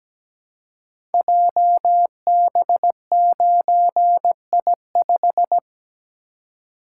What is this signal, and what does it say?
Morse code, heard as a keyed tone, about 17 wpm, 700 Hz. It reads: JB9I5
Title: Code Groups - 5 Letter:Numbers - 17wpm